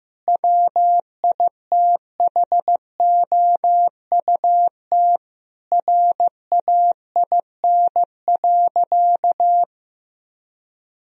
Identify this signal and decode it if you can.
Morse code, heard as a keyed tone, about 15 wpm, 700 Hz. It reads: WITHOUT RAIN.
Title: Sentences from Top 1000 Words - Review Part 3 - 15wpm